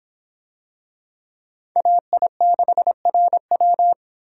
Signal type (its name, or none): Morse code